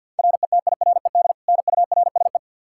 Morse code, heard as a keyed tone, about 47 wpm, 700 Hz. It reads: RETIRED NURSE
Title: Practice QSO - Exchange 3 - More Info - 50wpm